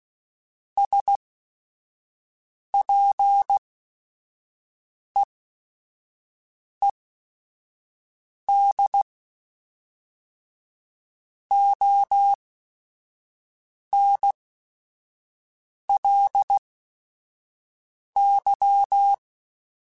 Morse code, heard as a keyed tone, about 16 wpm, spaced as for 5 wpm, 800 Hz. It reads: SPEED ONLY
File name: Weather Sets / Weather Sets-mp3s / Weather 4 28 @5